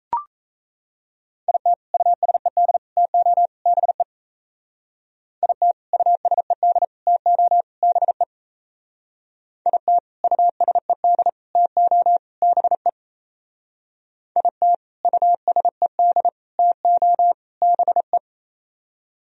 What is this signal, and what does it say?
Morse code, heard as a keyed tone, about 34 wpm, 700 Hz. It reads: IT USED TO BE IT USED TO BE IT USED TO BE IT USED TO BE
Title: Speed-Racing - Sets of 4 Words 28wpm